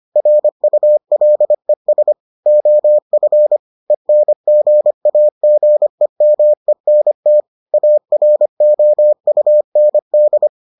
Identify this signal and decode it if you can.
Morse code, heard as a keyed tone, about 25 wpm, 600 Hz. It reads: RULES OF ENGAGEMENT AROUND